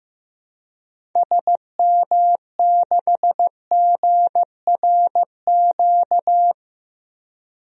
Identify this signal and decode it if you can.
Morse code, heard as a keyed tone, about 15 wpm, 700 Hz. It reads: SM6GRQ
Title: DXCC-100 Call Signs - 6 Characters - Repeated 2x - 15wpm